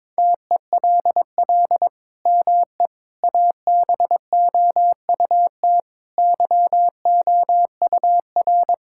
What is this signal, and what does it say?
Morse code, heard as a keyed tone, about 22 wpm, 700 Hz. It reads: TELL ME ABOUT YOUR
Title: Sentences from Top 1000 Words - Review Part 5 - 22wpm